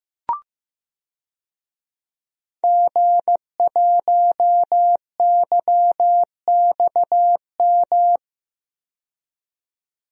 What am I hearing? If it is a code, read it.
Morse code, heard as a keyed tone, about 15 wpm, 700 Hz. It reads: G1YXM
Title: DXCC-100 Call Signs - 5 Characters - 15wpm